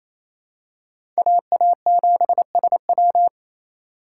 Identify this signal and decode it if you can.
Morse code, heard as a keyed tone, about 28 wpm, 700 Hz. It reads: AA7SW
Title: USA Call Signs - 5 Characters - 2x Character Spacing - 28wpm